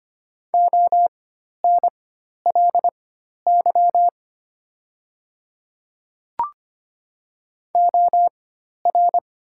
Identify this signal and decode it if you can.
Morse code, heard as a keyed tone, about 25 wpm, spaced as for 12 wpm, 700 Hz. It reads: ONLY OR
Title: 4x Character Spacing - Top 300 Words - 25wpm